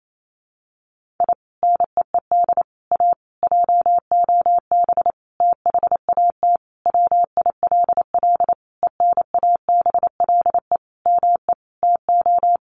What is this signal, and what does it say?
Morse code, heard as a keyed tone, about 28 wpm, 700 Hz. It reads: I NEED A JOB THAT WILL ENABLE ME TO
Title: Sentences from Top 1500 Words 28wpm